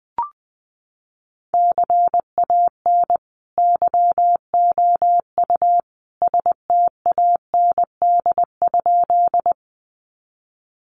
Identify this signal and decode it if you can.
Morse code, heard as a keyed tone, about 19 wpm, 700 Hz. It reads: CAN YOU STAND?
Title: Sentences from Top 300 Words 20wpm